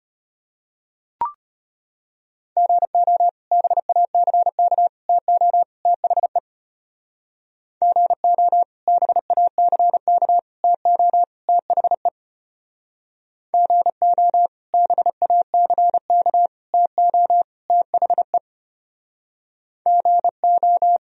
Morse code, heard as a keyed tone, about 32 wpm, 700 Hz. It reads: GO BACK TO THE GO BACK TO THE GO BACK TO THE GO